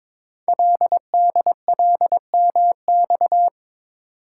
Morse code, heard as a keyed tone, about 23 wpm, 690 Hz. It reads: LDLMX
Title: Code Groups - 5 Letters - 22wpm